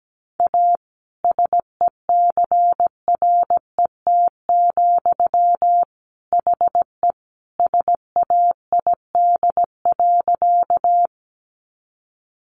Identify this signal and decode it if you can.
Morse code, heard as a keyed tone, about 17 wpm, 700 Hz. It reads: A SECRET, HE SAID.
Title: Sentences from Top 1200 Words 17wpm